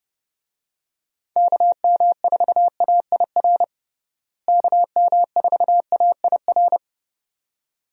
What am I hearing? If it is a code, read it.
Morse code, heard as a keyed tone, about 30 wpm, 700 Hz. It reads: KM4AIR KM4AIR
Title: USA Call Signs - 6 Characters - Repeated 2x - 30wpm